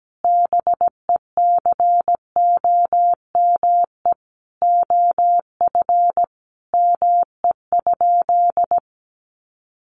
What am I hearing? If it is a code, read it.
Morse code, heard as a keyed tone, about 17 wpm, 700 Hz. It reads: BECOME OF ME?
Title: Sentences from Top 600 Words 17wpm